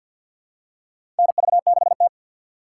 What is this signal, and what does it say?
Morse code, heard as a keyed tone, about 49 wpm, 700 Hz. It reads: NV6T
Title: USA Call Signs - 4 Characters - 3x Character Spacing - 50wpm